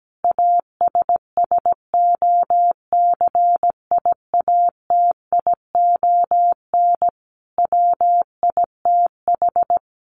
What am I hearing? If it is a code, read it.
Morse code, heard as a keyed tone, about 17 wpm, 700 Hz. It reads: ASSOCIATION WITH
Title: Sentences from Top 1100 Words 17wpm